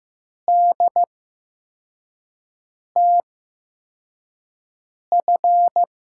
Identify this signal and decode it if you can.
Morse code, heard as a keyed tone, about 15 wpm, spaced as for 4 wpm, 700 Hz. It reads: DTF
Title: USA Call Signs - 5 Characters - 8x Character Spacing - 15wpm